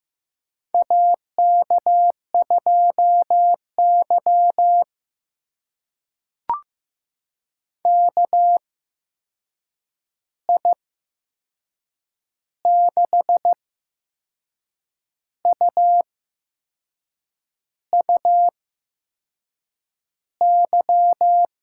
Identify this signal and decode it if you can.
Morse code, heard as a keyed tone, about 15 wpm, spaced as for 6 wpm, 700 Hz. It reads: AK2Y KI6UUY
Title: USA Call Signs - 8x Character Spacing - 15wpm